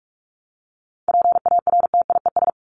Morse code, heard as a keyed tone, about 44 wpm, 700 Hz. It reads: PARTIES
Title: ES:S Suffix - Top 100 Words with Base Words 45wpm